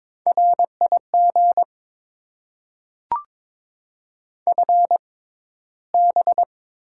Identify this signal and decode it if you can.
Morse code, heard as a keyed tone, about 22 wpm, spaced as for 14 wpm, 690 Hz. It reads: RIG FB
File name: Top 20 QSO Elements - 6x Character Spacing - 22wpm